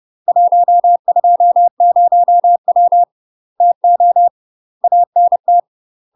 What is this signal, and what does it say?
Morse code, heard as a keyed tone, about 30 wpm, 700 Hz. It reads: 120W TO ANT
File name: UK MASTER QSOs TRAINING / 30WPM / UK MASTER QSOs 0019